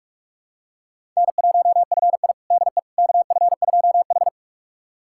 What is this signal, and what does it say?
Morse code, heard as a keyed tone, about 41 wpm, 700 Hz. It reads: N1FI DE KF2H
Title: USA Callsign DE Callsign - 4 Characters - 45wpm